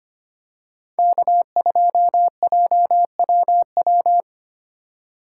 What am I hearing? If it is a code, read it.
Morse code, heard as a keyed tone, about 25 wpm, 690 Hz. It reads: K2JWW
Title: USA Call Signs - 5 Characters - 6x Character Spacing - 25wpm